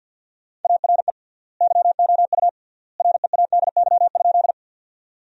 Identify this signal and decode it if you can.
Morse code, heard as a keyed tone, about 50 wpm, 700 Hz. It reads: ARE YOU READY?
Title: 3x Word-Spacing - Sentences from Top 400 Words - 50wpm